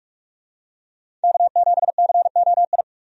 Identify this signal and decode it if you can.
Morse code, heard as a keyed tone, about 46 wpm, 700 Hz. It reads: K7COI
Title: USA Call Signs - 5 Characters - 45wpm